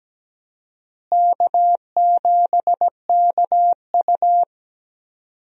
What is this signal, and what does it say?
Morse code, heard as a keyed tone, about 17 wpm, 700 Hz. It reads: K7KU
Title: USA Call Signs - 4 Characters - Repeated 2x - 17wpm